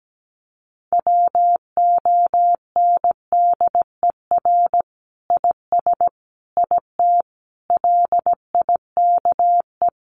Morse code, heard as a keyed tone, about 17 wpm, 700 Hz. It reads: WONDER IS IT LIKE
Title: Sentences from Top 500 Words 17wpm